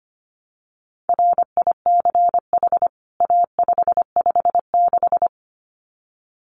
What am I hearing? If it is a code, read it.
Morse code, heard as a keyed tone, about 25 wpm, 700 Hz. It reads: RICH A556
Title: CWT Exchange 25wpm